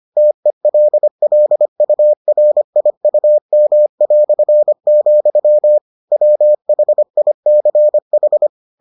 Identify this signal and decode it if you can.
Morse code, heard as a keyed tone, about 25 wpm, 600 Hz. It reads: TELLURIUM", WHICH